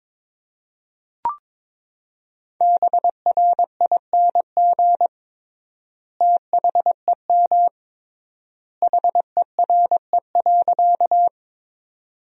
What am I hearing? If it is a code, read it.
Morse code, heard as a keyed tone, about 22 wpm, 700 Hz. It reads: BRING THEM HERE.
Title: 3x Word-Spacing - Sentences from Top 500 Words - 22wpm